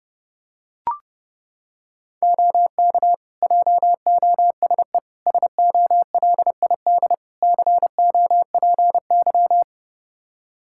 Morse code, heard as a keyed tone, about 30 wpm, 700 Hz. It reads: OK JOSE SOLID COPY
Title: K1USN SST QSO Practice 30wpm